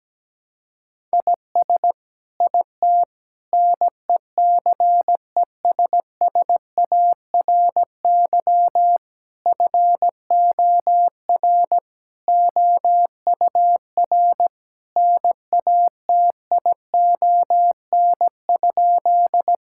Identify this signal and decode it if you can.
Morse code, heard as a keyed tone, about 17 wpm, 700 Hz. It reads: IS IT NECESSARY FOR OUR NATION?
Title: Sentences from Top 1000 Words - Review Part 6 - 17wpm